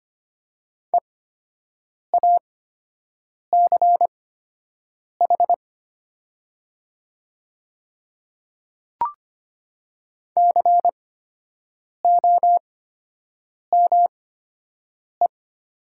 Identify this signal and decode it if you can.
Morse code, heard as a keyed tone, about 25 wpm, spaced as for 7 wpm, 700 Hz. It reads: EACH COME
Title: 8x Character Spacing - Top 100 Words - 25wpm